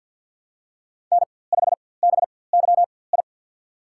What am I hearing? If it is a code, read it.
Morse code, heard as a keyed tone, about 52 wpm, spaced as for 24 wpm, 700 Hz. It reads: N5BYI
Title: USA Call Signs - 4x Character Spacing - 50wpm